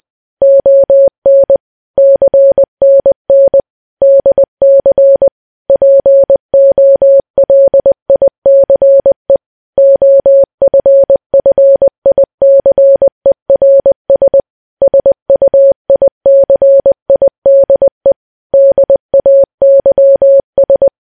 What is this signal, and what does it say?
Morse code, heard as a keyed tone, about 20 wpm, 550 Hz. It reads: ON CNN DC POLICE OFFICERS SUICIDE DAYS
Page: news-headlines-in-morse-code-at-20-wpm